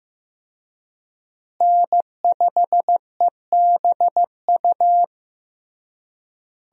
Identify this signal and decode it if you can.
Morse code, heard as a keyed tone, about 16 wpm, 700 Hz. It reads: N5EBU
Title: USA Call Signs - 5 Characters - Repeated 3x - 15wpm